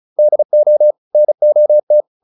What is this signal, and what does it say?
Morse code, heard as a keyed tone, about 34 wpm, 600 Hz. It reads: DO NOT